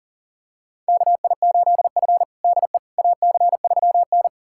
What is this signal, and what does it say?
Morse code, heard as a keyed tone, about 40 wpm, 700 Hz. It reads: KI8F DE AC3N
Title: USA Callsign DE Callsign - 4 Characters - 40wpm